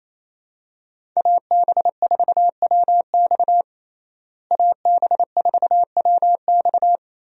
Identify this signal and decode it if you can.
Morse code, heard as a keyed tone, about 27 wpm, 700 Hz. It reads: AB4WX AB4WX
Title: USA Call Signs - 5 Characters - Repeated 3x - 28wpm